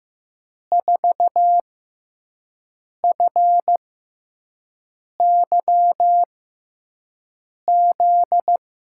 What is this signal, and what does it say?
Morse code, heard as a keyed tone, about 15 wpm, spaced as for 5 wpm, 700 Hz. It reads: 4FYZ
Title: USA Call Signs - 6 Characters - 6x Character Spacing - 15wpm